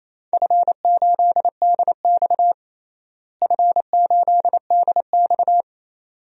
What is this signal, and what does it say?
Morse code, heard as a keyed tone, about 27 wpm, 700 Hz. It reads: F8DX F8DX
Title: DXCC-100 Call Signs - 4 Characters - Repeated 2x - 28wpm